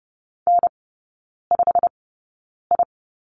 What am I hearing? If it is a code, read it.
Morse code, heard as a keyed tone, about 30 wpm, spaced as for 9 wpm, 700 Hz. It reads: N5I
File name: Lesson 16e - Letter c - Three Characters 30wpm